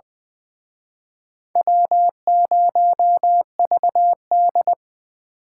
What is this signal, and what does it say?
Morse code, heard as a keyed tone, about 23 wpm, 700 Hz. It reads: W0VD
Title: Speed Racing - USA 1x2 Call Signs 20wpm